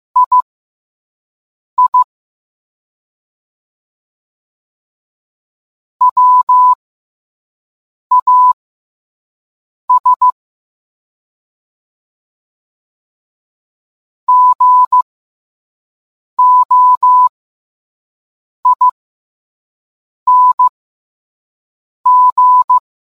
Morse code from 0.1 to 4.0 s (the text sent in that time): II